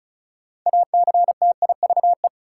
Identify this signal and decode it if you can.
Morse code, heard as a keyed tone, about 36 wpm, 700 Hz. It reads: ACTIVE
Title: Sentences from Top 900 Words - New Word Drill - 35wpm